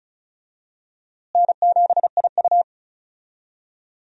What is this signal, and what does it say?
Morse code, heard as a keyed tone, about 36 wpm, 700 Hz. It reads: N7IU
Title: USA Call Signs - 4 Characters - Repeated 3x - 35wpm